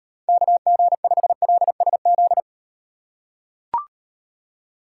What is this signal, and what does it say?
Morse code, heard as a keyed tone, about 38 wpm, 700 Hz. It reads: KG5LSZ